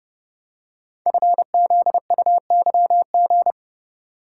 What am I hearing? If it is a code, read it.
Morse code, heard as a keyed tone, about 30 wpm, 700 Hz. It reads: FZUYG
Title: Code Groups - 5 Letter:Numbers - 30wpm